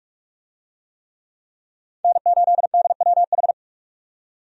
Morse code, heard as a keyed tone, about 47 wpm, 700 Hz. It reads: N8DWH
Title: USA Call Signs - 5 Characters - 45wpm